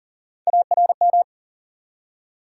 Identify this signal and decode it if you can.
Morse code, heard as a keyed tone, about 42 wpm, 700 Hz. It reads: ARM
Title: Three Letter Words - 40wpm